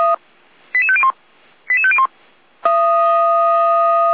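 Five-tone selective call (ZVEI), heard as quick runs of five tones, tones of about 70 ms. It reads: 88501 88501
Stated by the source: ZVEI five-tone call 88192 received on an SDR.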